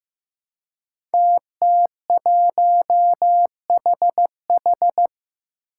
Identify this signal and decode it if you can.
Morse code, heard as a keyed tone, about 15 wpm, 700 Hz. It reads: TT1HH
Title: Five Challenging Letter-Number Combinations 15wpm